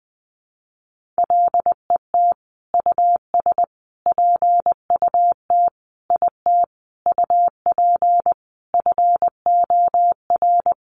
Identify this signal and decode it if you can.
Morse code, heard as a keyed tone, about 20 wpm, 700 Hz. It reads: LET US PUT IT UP FOR
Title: Sentences from Top 1400 Words 20wpm